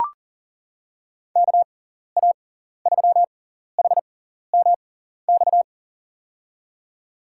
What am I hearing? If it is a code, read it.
Morse code, heard as a keyed tone, about 40 wpm, spaced as for 14 wpm, 700 Hz. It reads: KA3HMX
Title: USA Call Signs - 6 Characters - 6x Character Spacing - 40wpm